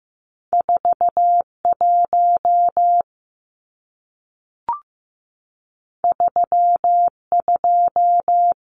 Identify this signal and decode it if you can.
Morse code, heard as a keyed tone, about 15 wpm, 700 Hz. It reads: 41 32
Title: Numbers - 2-Digit 15wpm